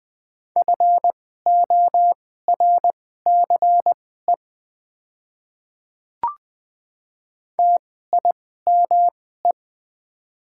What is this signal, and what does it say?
Morse code, heard as a keyed tone, about 20 wpm, spaced as for 15 wpm, 700 Hz. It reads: FORCE TIME
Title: 2x Character Spacing - Top 500 Words - 20wpm